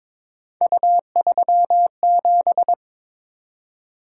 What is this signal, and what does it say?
Morse code, heard as a keyed tone, about 22 wpm, 700 Hz. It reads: U37
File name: practice / 22wpm / Three Letter-Number Combinations - 22wpm